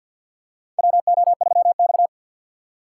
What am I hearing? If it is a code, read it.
Morse code, heard as a keyed tone, about 46 wpm, 700 Hz. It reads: WO3X
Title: USA Call Signs - 4 Characters - 6x Character Spacing - 50wpm